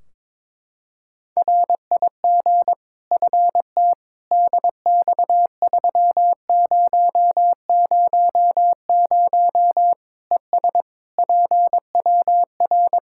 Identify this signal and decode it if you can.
Morse code, heard as a keyed tone, about 22 wpm, 700 Hz. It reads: RIG FT DX3000 ES PWR
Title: Practice QSO - Exchange 2 - Info - 22wpm